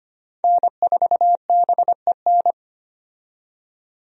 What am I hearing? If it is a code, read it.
Morse code, heard as a keyed tone, about 25 wpm, 700 Hz. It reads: N4BEN
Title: USA Call Signs - 5 Characters - 25wpm